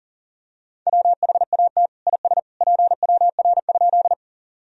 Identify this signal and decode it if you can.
Morse code, heard as a keyed tone, about 40 wpm, 700 Hz. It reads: WHAT IS PWR?
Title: Sentences from Top 500 Words and Abbreviations 40wpm